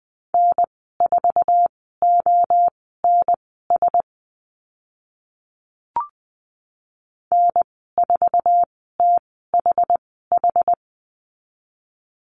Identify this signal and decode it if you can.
Morse code, heard as a keyed tone, about 20 wpm, spaced as for 14 wpm, 700 Hz. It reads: N4ONS N4THH
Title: 076 Lesson 13g - Letter L - Call Signs 20wpm